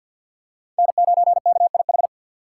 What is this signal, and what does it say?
Morse code, heard as a keyed tone, about 52 wpm, 700 Hz. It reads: N9KIH
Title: USA Call Signs - 3x Character Spacing - 50wpm